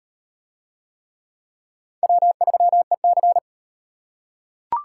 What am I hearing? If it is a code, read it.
Morse code, heard as a keyed tone, about 36 wpm, 700 Hz. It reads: W3EC